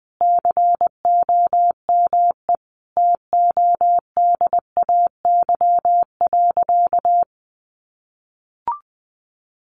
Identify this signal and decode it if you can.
Morse code, heard as a keyed tone, about 20 wpm, 700 Hz. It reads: COME TODAY.
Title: Additional Easy Phrases 20wpm